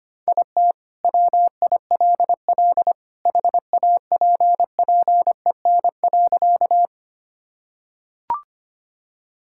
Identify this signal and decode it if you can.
Morse code, heard as a keyed tone, about 25 wpm, 700 Hz. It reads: IT WILL HAPPEN.
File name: practice / 25wpm / Sentences with 2-Word Phrases - 25wpm